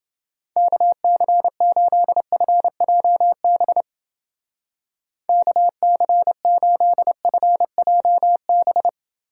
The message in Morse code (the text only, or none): KC8FJB KC8FJB